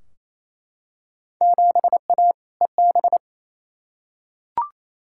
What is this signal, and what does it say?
Morse code, heard as a keyed tone, about 28 wpm, 700 Hz. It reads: 7A EB
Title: Field Day Exchanges Encore - 28wpm